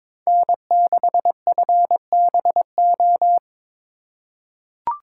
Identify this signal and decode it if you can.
Morse code, heard as a keyed tone, about 22 wpm, 700 Hz. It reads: N6FBO
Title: USA Call Signs - 5 Characters - 4x Character Spacing - 22wpm